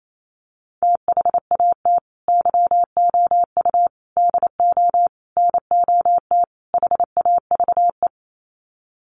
Morse code, heard as a keyed tone, about 28 wpm, 700 Hz. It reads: THAT YOU DO NOT HAVE
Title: Sets of 5 Words Encore 28wpm